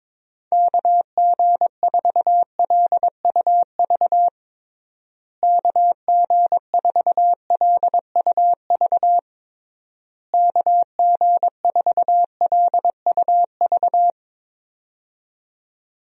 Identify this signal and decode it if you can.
Morse code, heard as a keyed tone, about 22 wpm, 700 Hz. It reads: KG4LUV KG4LUV KG4LUV
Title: USA Call Signs - 6 Characters - Repeated 3x - 22wpm